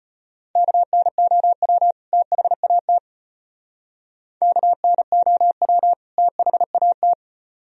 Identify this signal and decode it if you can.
Morse code, heard as a keyed tone, about 31 wpm, 700 Hz. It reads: KNOW THAT KNOW THAT